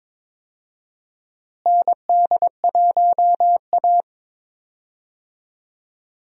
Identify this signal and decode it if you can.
Morse code, heard as a keyed tone, about 22 wpm, 700 Hz. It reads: ND1A
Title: USA Call Signs - 4 Characters - Repeated 3x - 22wpm